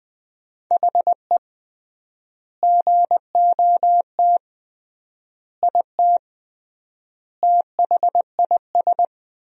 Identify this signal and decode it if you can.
Morse code, heard as a keyed tone, about 20 wpm, 700 Hz. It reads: HE GOT IT THIS
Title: 3x Word-Spacing - Sentences from Top 400 Words - 20wpm